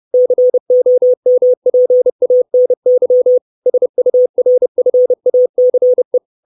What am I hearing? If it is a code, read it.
Morse code, heard as a keyed tone, about 30 wpm, 500 Hz. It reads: COMPANY SURFACE
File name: 632 words / 632 Words 30WPM